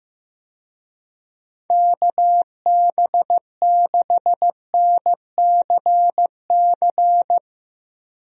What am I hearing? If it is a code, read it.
Morse code, heard as a keyed tone, about 16 wpm, 700 Hz. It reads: KB6NCC
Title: USA Call Signs - 6 Characters - Repeated 2x - 15wpm